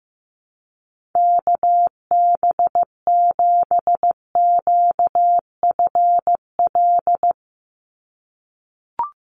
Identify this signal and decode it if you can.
Morse code, heard as a keyed tone, about 15 wpm, 700 Hz. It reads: KB7QFL
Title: USA Call Signs 15wpm